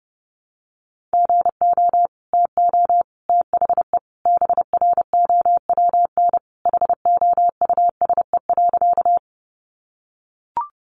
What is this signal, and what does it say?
Morse code, heard as a keyed tone, about 30 wpm, 700 Hz. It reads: GO TO THE BROWN HOUSE.
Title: Sentences from Top 1200 Words 30wpm